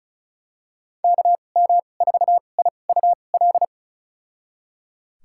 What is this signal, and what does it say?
Morse code, heard as a keyed tone, about 36 wpm, spaced as for 25 wpm, 700 Hz. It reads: KM4IUL
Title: USA Call Signs - 2x Character Spacing - 35wpm